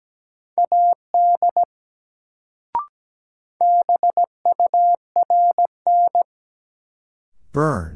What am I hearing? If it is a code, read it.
Morse code, heard as a keyed tone, about 17 wpm, 700 Hz. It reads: AD BURN
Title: Sentences from Top 1300 Words - New Word Drill - 17wpm